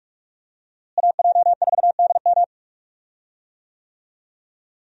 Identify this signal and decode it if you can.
Morse code, heard as a keyed tone, about 44 wpm, 700 Hz. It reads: AJ4DM
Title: USA Call Signs - 5 Characters - Repeated 2x - 45wpm